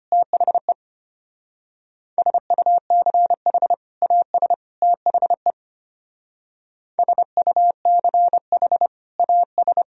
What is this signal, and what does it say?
Morse code, heard as a keyed tone, about 30 wpm, 700 Hz. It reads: THE SUCH AS THE SUCH AS
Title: Speed-Racing - Sets of 3 Words 25wpm